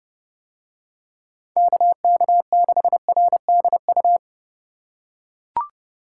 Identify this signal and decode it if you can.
Morse code, heard as a keyed tone, about 32 wpm, 700 Hz. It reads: KK6RDU